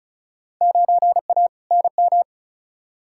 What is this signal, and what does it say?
Morse code, heard as a keyed tone, about 36 wpm, 700 Hz. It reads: 9A NM
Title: Field Day Exchanges Encore - 35wpm